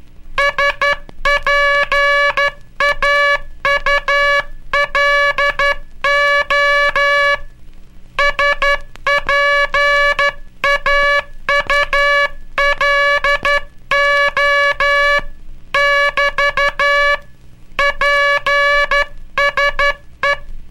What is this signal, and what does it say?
Morse code, heard as a keyed tone, about 12 wpm, 1700 Hz. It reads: SPAULO SPAULO <BT> PSE
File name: cursos / Curso CW mp3 / CW Aula - 18